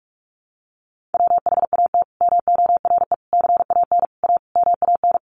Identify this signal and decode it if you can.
Morse code, heard as a keyed tone, about 44 wpm, 700 Hz. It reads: WHAT MORE CAN A MAN
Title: Sentences from Top 1100 Words 45wpm